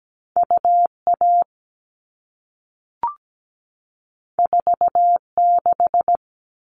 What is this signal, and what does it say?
Morse code, heard as a keyed tone, about 17 wpm, 700 Hz. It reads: UA 46
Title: All Cut Numbers - 2-Digit 17wpm